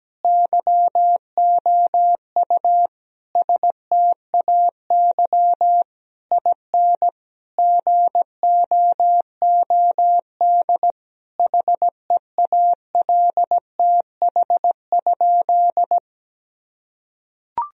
Morse code, heard as a keyed tone, about 17 wpm, 700 Hz. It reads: YOU STAY IN GOOD HEALTH?
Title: Sentences from Top 1000 Words - Review Part 3 - 17wpm